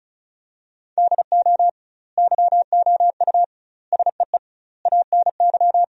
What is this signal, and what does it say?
Morse code, heard as a keyed tone, about 34 wpm, 700 Hz. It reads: DO YOU SEE ANY
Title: 2x Word-Spacing - Sentences from Top 500 Words - 35wpm